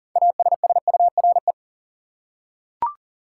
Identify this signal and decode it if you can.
Morse code, heard as a keyed tone, about 37 wpm, 700 Hz. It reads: ASSURE
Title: Six Letter Words - 40wpm